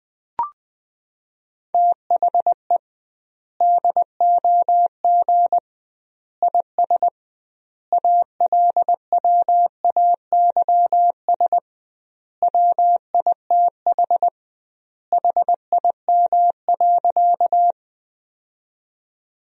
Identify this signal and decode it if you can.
Morse code, heard as a keyed tone, about 20 wpm, 700 Hz. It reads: THE DOG IS ALWAYS WITH HIM.
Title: 2x Word-Spacing - Sentences from Top 400 Words - 20wpm